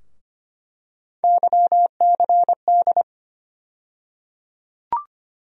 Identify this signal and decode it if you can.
Morse code, heard as a keyed tone, about 25 wpm, 700 Hz. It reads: YCD
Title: USA Call Sign Postfixes - 3 Letters 25wpm